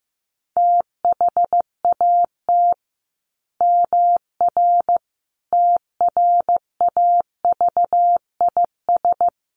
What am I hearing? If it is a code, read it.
Morse code, heard as a keyed tone, about 15 wpm, 700 Hz. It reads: THAT MR TRAVIS
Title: A Sound Of Thunder 15wpm